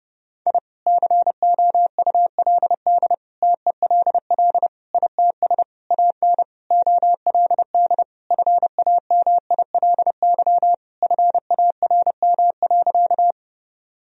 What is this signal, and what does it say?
Morse code, heard as a keyed tone, about 32 wpm, 700 Hz. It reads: I COULD TELL ITS AN OLD FAMILY FARM.
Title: Sentences from Top 1000 Words - Review Part 2 - 30wpm